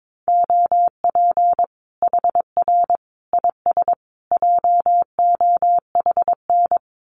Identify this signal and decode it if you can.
Morse code, heard as a keyed tone, about 22 wpm, 700 Hz. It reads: OP HR IS JOHN
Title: Common QSO Phrases 22wpm